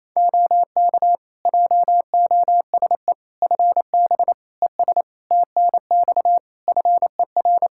Morse code, heard as a keyed tone, about 28 wpm, 700 Hz. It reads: OK JOSE FB ES TNX FER
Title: K1USN SST QSO Practice 28wpm